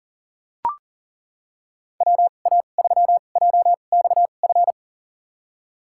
Morse code, heard as a keyed tone, about 40 wpm, spaced as for 29 wpm, 700 Hz. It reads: WA3JXF
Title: USA Call Signs - 2x Character Spacing - 40wpm